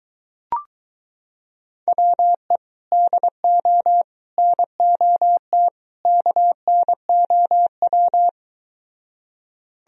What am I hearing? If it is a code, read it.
Morse code, heard as a keyed tone, about 23 wpm, 700 Hz. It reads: WE DO NOT KNOW